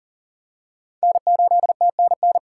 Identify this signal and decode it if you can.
Morse code, heard as a keyed tone, about 40 wpm, 700 Hz. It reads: N8TNN
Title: USA Call Signs - 5 Characters - 4x Character Spacing - 40wpm